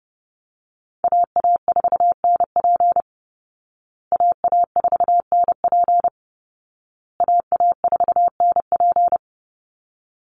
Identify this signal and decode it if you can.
Morse code, heard as a keyed tone, about 30 wpm, 700 Hz. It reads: AA4NP AA4NP AA4NP
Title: Call Signs Repeated 2x - 30wpm